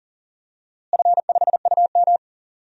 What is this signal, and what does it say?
Morse code, heard as a keyed tone, about 39 wpm, 700 Hz. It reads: F5UM
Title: DXCC-100 Call Signs - 4 Characters - Repeated 2x - 40wpm